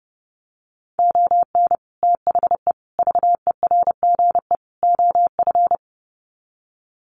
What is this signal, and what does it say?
Morse code, heard as a keyed tone, about 30 wpm, 700 Hz. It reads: ON THE VERGE OF
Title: Sets of 4 Words 30wpm